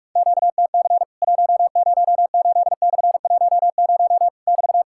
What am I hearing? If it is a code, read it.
Morse code, heard as a keyed tone, about 46 wpm, 700 Hz. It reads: QTC 108/10 <BT>
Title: Worked All Europe DX Contest - QTC Exchange - 45wpm